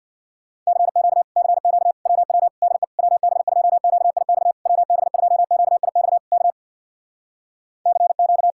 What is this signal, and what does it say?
Morse code, heard as a keyed tone, about 62 wpm, 700 Hz. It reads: CQ CQ WW DE WB2QIX WB2QIX K CQ
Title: Speed-Racing Common QSO Phrases - 50wpm